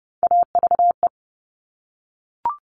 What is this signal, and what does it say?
Morse code, heard as a keyed tone, about 30 wpm, 700 Hz. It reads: AVE
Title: All Cut Numbers - 3-Digit 30wpm